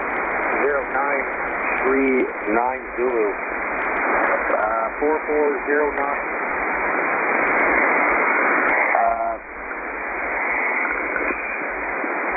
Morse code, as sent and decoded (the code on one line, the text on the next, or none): none